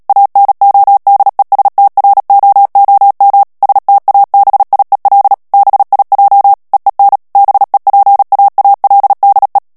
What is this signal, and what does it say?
Morse code, heard as a keyed tone, about 37 wpm, 800 Hz. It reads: ANODESTROOM STABIEL BIJ EEN BEPAALDE
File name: morse mp3 / HOLLANDAIS texte clair 2 vitesse 35 mots minute